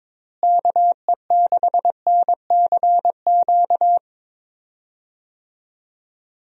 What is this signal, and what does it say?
Morse code, heard as a keyed tone, about 22 wpm, 700 Hz. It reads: KE6NCQ
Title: USA Call Signs - 6 Characters - Repeated 2x - 22wpm